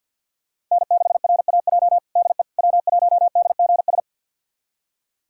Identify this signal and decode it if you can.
Morse code, heard as a keyed tone, about 50 wpm, 700 Hz. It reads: N6RAJ DE W1DGS
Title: USA Callsign DE Callsign - 5 Characters - 50wpm